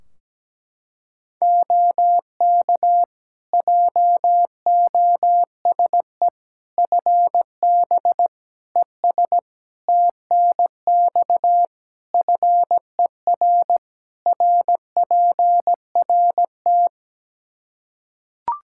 Morse code, heard as a keyed tone, about 17 wpm, 700 Hz. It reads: OK JOSE FB ES TNX FER RPRT
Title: K1USN SST QSO Practice 17wpm